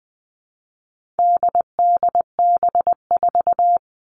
Morse code, heard as a keyed tone, about 20 wpm, 700 Hz. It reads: DDB4
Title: All Cut Numbers - 4-Digit 20wpm